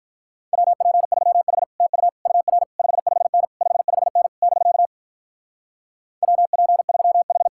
Morse code, heard as a keyed tone, about 43 wpm, 700 Hz. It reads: WP3H TU UR 55N 55N <BK> WP3H
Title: Speed-Racing Common QSO Phrases Encore - 35wpm